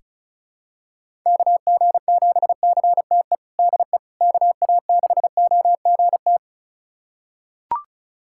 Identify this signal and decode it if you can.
Morse code, heard as a keyed tone, about 36 wpm, 700 Hz. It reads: KG7CTE DE KA6OGT
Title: USA Callsign DE Callsign - 6 Characters - 35wpm